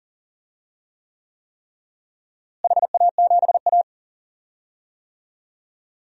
Morse code, heard as a keyed tone, about 40 wpm, 700 Hz. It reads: HA7A
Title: DXCC-100 Call Signs - 4 Characters - Repeated 3x - 40wpm